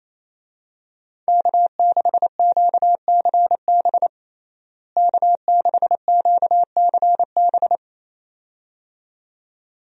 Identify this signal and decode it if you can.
Morse code, heard as a keyed tone, about 27 wpm, 700 Hz. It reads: K6QCB K6QCB
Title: USA Call Signs - 5 Characters - Repeated 2x - 28wpm